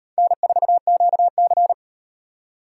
Morse code, heard as a keyed tone, about 38 wpm, 700 Hz. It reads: N4QC